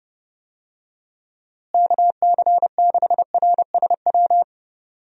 Morse code, heard as a keyed tone, about 32 wpm, 700 Hz. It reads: KC6RSW
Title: USA Call Signs - 6 Characters - Repeated 3x - 30wpm